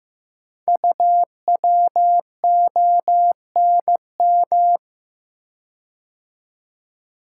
Morse code, heard as a keyed tone, about 15 wpm, 700 Hz. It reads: UWONM
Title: Code Groups - 5 Letters - 15wpm